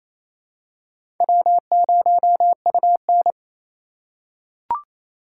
Morse code, heard as a keyed tone, about 28 wpm, 700 Hz. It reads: W0UN
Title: USA Call Signs - 4 Characters - 2x Character Spacing - 28wpm